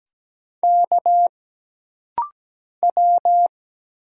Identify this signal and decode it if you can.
Morse code, heard as a keyed tone, about 17 wpm, 700 Hz. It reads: K W
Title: Single Letter-Number - Rapid-Fire 17wpm